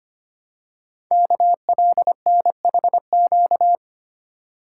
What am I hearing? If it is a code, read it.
Morse code, heard as a keyed tone, about 25 wpm, 700 Hz. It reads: KLNHQ
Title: Code Groups - 5 Letters - 25wpm